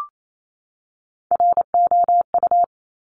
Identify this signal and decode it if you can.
Morse code, heard as a keyed tone, about 28 wpm, 700 Hz. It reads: ROU
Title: Top Three Letter Combinations in Words 28wpm